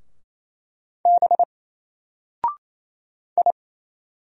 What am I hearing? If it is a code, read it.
Morse code, heard as a keyed tone, about 28 wpm, 700 Hz. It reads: B I
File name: Single-Letter-Number-28wpm